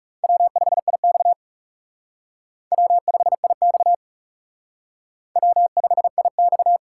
Morse code, heard as a keyed tone, about 37 wpm, 700 Hz. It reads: W5IX W5IX W5IX
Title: Speed Racing - USA 1x2 Call Signs 30wpm